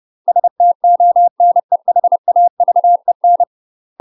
Morse code, heard as a keyed tone, about 30 wpm, 700 Hz. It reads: STONEHAVEN